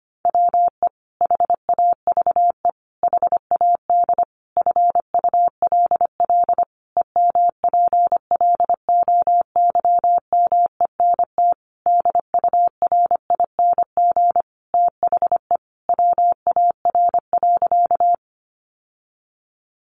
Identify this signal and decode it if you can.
Morse code, heard as a keyed tone, about 25 wpm, 700 Hz. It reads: WE HAVE HAD FULL EMPLOYMENT DURING THE WAR.
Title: Sentences from Top 1500 Words 25wpm